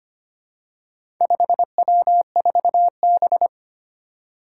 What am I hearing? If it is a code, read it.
Morse code, heard as a keyed tone, about 25 wpm, 700 Hz. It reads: 5W4B
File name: practice / 25wpm / Four Letter-Number Combinations - 25wpm